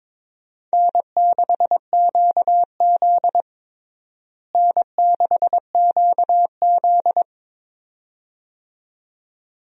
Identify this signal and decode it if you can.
Morse code, heard as a keyed tone, about 21 wpm, 700 Hz. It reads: N6QZ N6QZ
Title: USA Call Signs - 4 Characters - Repeated 2x - 22wpm